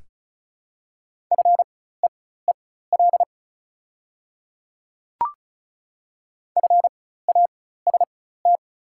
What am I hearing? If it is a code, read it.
Morse code, heard as a keyed tone, about 34 wpm, spaced as for 16 wpm, 690 Hz. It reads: FEEL FAST
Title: 4x Character Spacing - Top 500 Words - 35wpm